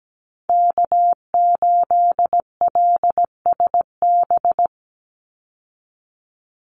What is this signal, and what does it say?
Morse code, heard as a keyed tone, about 17 wpm, 700 Hz. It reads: K8LSB
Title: Call Signs Repeated 3x - 17wpm